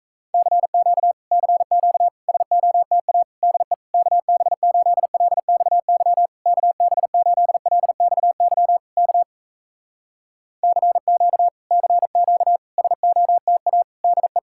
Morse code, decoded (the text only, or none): CQ CQ SOTA DE KB8LXY KB8LXY K CQ CQ SOTA DE